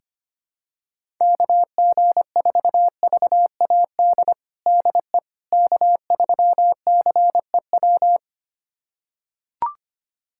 Morse code, heard as a keyed tone, about 25 wpm, 700 Hz. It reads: KG4VAD DE K3CEW
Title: USA Callsign DE Callsign 25wpm